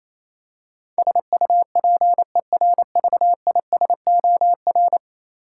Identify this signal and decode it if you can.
Morse code, heard as a keyed tone, about 28 wpm, 700 Hz. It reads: SUPERVISOR
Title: Practice QSO - Exchange 3 - More Info - 28wpm